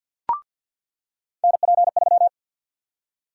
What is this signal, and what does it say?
Morse code, heard as a keyed tone, about 47 wpm, 700 Hz. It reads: NP3
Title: USA Call Sign Prefixes - 50wpm